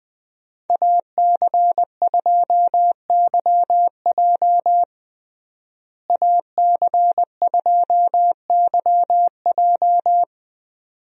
Morse code, heard as a keyed tone, about 20 wpm, 700 Hz. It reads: AC2YJ AC2YJ
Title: USA Call Signs - 5 Characters - Repeated 2x - 20wpm